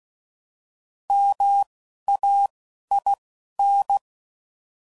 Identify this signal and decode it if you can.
Morse code, heard as a keyed tone, about 16 wpm, spaced as for 11 wpm, 800 Hz. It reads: MAIN